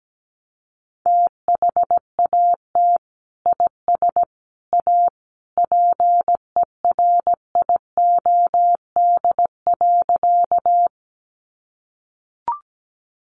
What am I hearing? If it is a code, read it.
Morse code, heard as a keyed tone, about 17 wpm, 700 Hz. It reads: THAT IS A PERIOD.
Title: Sentences from Top 800 Words 17wpm